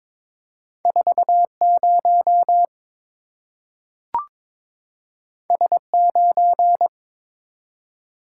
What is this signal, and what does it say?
Morse code, heard as a keyed tone, about 22 wpm, 690 Hz. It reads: 40 S9
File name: practice / 22wpm / Two Letter-Number Combinations - 22wpm